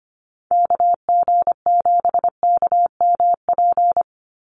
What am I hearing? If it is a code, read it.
Morse code, heard as a keyed tone, about 25 wpm, 700 Hz. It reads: KG7KMP
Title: USA Call Signs 25wpm